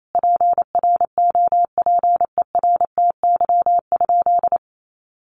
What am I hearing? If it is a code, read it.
Morse code, heard as a keyed tone, about 28 wpm, 700 Hz. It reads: PROPERTY?
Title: Sentences from Top 1500 Words 28wpm